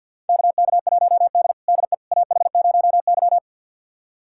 Morse code, heard as a keyed tone, about 52 wpm, 700 Hz. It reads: KK1D DE AH0Y
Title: USA Callsign DE Callsign - 4 Characters - 50wpm